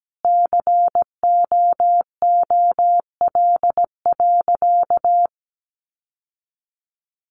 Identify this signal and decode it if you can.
Morse code, heard as a keyed tone, about 17 wpm, 700 Hz. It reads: COOL.
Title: Sentences from Top 1200 Words 17wpm